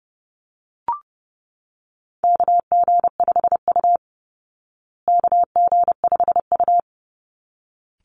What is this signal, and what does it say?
Morse code, heard as a keyed tone, about 30 wpm, 700 Hz. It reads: KG5U KG5U
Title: Call Signs Repeated 3x - 30wpm